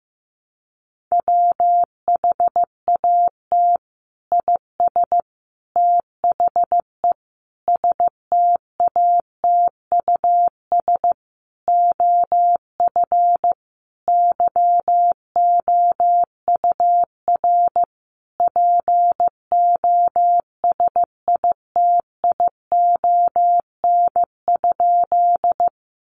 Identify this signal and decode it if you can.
Morse code, heard as a keyed tone, about 15 wpm, 700 Hz. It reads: WHAT IS THE STATUS OF YOUR POSITION?
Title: Sentences from Top 1400 Words 15wpm